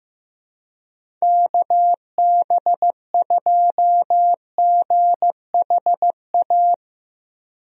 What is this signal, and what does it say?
Morse code, heard as a keyed tone, about 15 wpm, 700 Hz. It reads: KB2GHA
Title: USA Call Signs - 6 Characters - 15wpm